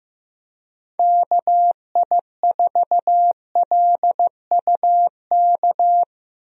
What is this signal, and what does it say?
Morse code, heard as a keyed tone, about 15 wpm, 700 Hz. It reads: KI4LUK
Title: USA Call Signs - 6 Characters - 15wpm